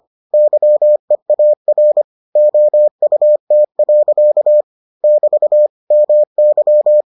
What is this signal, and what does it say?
Morse code, heard as a keyed tone, about 25 wpm, 600 Hz. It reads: YEAR OUT. <BT> MY